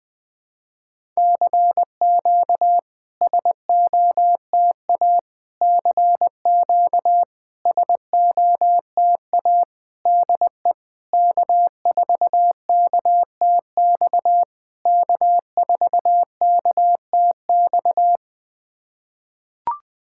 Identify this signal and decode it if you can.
Morse code, heard as a keyed tone, about 20 wpm, 690 Hz. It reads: CQ SOTA CQ SOTA DE K4KTX K4KTX
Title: SOTA QSO Practice 20wpm